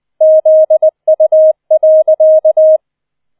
Morse code, heard as a keyed tone, about 20 wpm, 630 Hz. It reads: ZU.